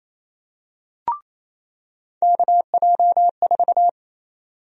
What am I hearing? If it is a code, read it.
Morse code, heard as a keyed tone, about 28 wpm, 700 Hz. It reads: KJ4
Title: USA Call Sign Prefixes - 28wpm